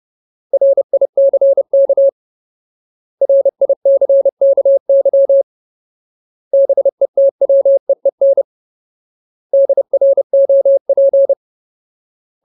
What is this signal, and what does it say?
Morse code, heard as a keyed tone, about 30 wpm, 550 Hz. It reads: RICK RICKY BETWEEN DROP
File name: Ultimate Word List / 30WPM Ultimate